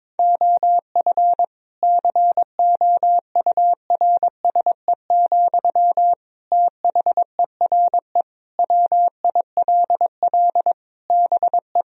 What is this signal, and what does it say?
Morse code, heard as a keyed tone, about 22 wpm, 700 Hz. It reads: OF COURSE, THERE WILL BE
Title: Sentences from Top 1000 Words - Review Part 4 - 22wpm